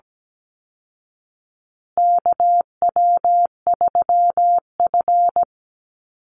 Morse code, heard as a keyed tone, about 17 wpm, 700 Hz. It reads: KW3F
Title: Call Signs Repeated 2x - 17wpm